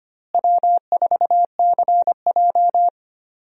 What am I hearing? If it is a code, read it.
Morse code, heard as a keyed tone, about 25 wpm, 700 Hz. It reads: W4CJ
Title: USA Call Signs - 4 Characters - Repeated 2x - 25wpm